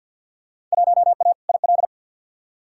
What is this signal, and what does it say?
Morse code, heard as a keyed tone, about 51 wpm, 700 Hz. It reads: 1A IL
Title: Field Day Exchanges Encore - 50wpm